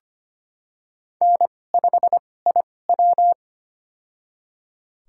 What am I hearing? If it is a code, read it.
Morse code, heard as a keyed tone, about 25 wpm, spaced as for 18 wpm, 700 Hz. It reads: N5IW
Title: USA Call Signs - 4 Characters - 2x Character Spacing - 25wpm